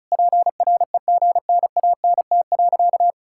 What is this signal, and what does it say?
Morse code, heard as a keyed tone, about 34 wpm, 700 Hz. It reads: PREGNANT.
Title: Sentences from Top 1000 Words - Review Part 6 - 35wpm